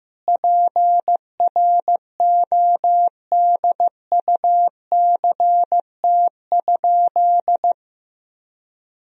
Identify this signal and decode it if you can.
Morse code, heard as a keyed tone, about 15 wpm, 700 Hz. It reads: PRODUCT?
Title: Sentences from Top 1000 Words - Review Part 3 - 15wpm